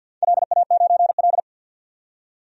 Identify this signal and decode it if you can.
Morse code, heard as a keyed tone, about 52 wpm, 700 Hz. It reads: LA9L
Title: DXCC-100 Call Signs - 4 Characters - 50wpm